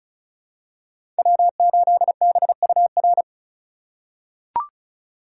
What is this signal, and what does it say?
Morse code, heard as a keyed tone, about 36 wpm, 700 Hz. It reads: W8BUR
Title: USA Call Signs - 5 Characters - 3x Character Spacing - 35wpm